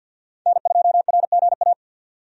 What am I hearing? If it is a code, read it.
Morse code, heard as a keyed tone, about 46 wpm, 700 Hz. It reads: N2RGA
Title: USA Call Signs - 5 Characters - 50wpm